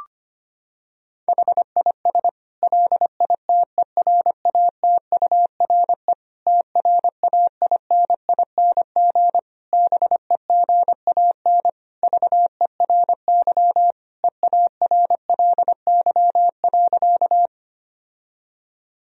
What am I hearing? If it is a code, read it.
Morse code, heard as a keyed tone, about 25 wpm, 700 Hz. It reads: HIS LITERATURE TRAINING BEGAN VERY EARLY.
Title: Sentences from Top 1000 Words - Review Part 4 - 25wpm